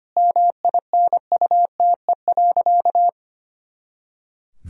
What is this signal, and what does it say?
Morse code, heard as a keyed tone, about 25 wpm, 700 Hz. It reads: MINUTE.
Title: Sentences from Top 500 Words and Abbreviations 25wpm